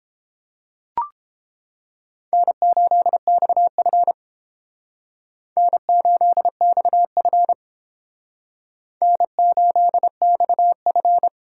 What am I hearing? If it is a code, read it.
Morse code, heard as a keyed tone, about 30 wpm, 700 Hz. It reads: N8XF N8XF N8XF